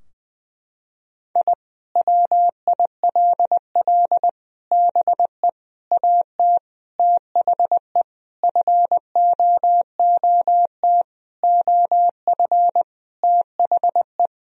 Morse code, heard as a keyed tone, about 21 wpm, 700 Hz. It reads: I WILL BE AT THE FOOT OF THE
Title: Sentences from Top 1000 Words - Review Part 3 - 20wpm